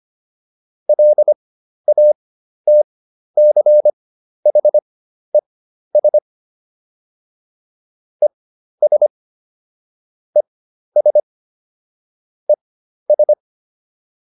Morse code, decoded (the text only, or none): LATCHES ES ES ES